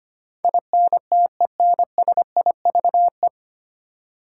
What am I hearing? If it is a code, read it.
Morse code, heard as a keyed tone, about 25 wpm, 700 Hz. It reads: INTENSIVE
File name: practice / 25wpm / 9 Letter Words - First 3 Letters Given - 25wpm